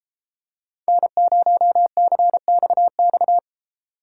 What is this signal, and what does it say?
Morse code, heard as a keyed tone, about 33 wpm, 700 Hz. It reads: N0CXX